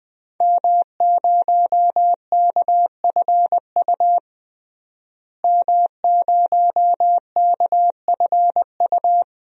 Morse code, heard as a keyed tone, about 21 wpm, 700 Hz. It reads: M0KFU M0KFU
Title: DXCC-100 Call Signs - 5 Characters - Repeated 2x - 20wpm